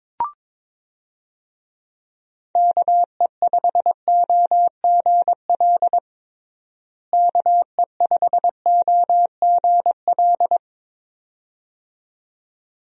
Morse code, heard as a keyed tone, about 22 wpm, 700 Hz. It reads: KE5OGL KE5OGL
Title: USA Call Signs - 6 Characters - Repeated 2x - 22wpm